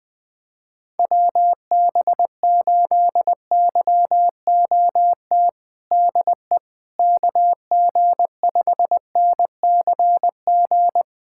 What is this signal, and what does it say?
Morse code, heard as a keyed tone, about 21 wpm, 700 Hz. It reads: WB8YOT DE KG5NCG
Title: USA Callsign DE Callsign 20wpm